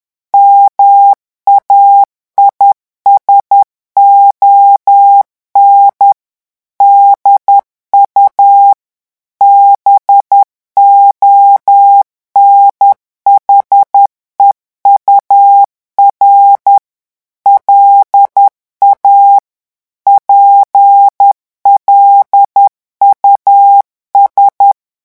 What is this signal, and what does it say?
Morse code, heard as a keyed tone, about 10 wpm, 800 Hz. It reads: MAISON DU BONHEUR LA PLUS
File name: morse mp3 / revision francais 10 mots minute